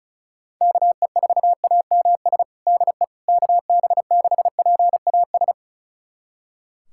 Morse code, heard as a keyed tone, about 34 wpm, 700 Hz. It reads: KE4AMS DE KB6PAS
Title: USA Callsign DE Callsign 35wpm